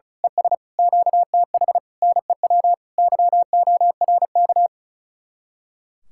Morse code, heard as a keyed tone, about 33 wpm, 700 Hz. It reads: ES QTH NEW YORK
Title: Speed-Racing Common QSO Phrases Encore - 35wpm